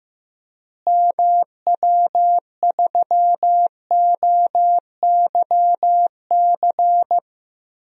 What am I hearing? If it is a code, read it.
Morse code, heard as a keyed tone, about 15 wpm, 700 Hz. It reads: MW3OYC
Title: DXCC-100 Call Signs - 6 Characters - Repeated 2x - 15wpm